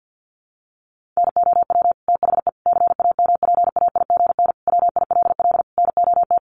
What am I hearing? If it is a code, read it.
Morse code, heard as a keyed tone, about 49 wpm, 700 Hz. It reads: NOW THE CAMPAIGN WILL NOT
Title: Sentences from Top 1200 Words 50wpm